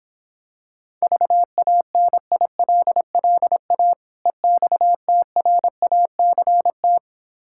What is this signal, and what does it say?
Morse code, heard as a keyed tone, about 26 wpm, 700 Hz. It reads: VANILLA EXTRACT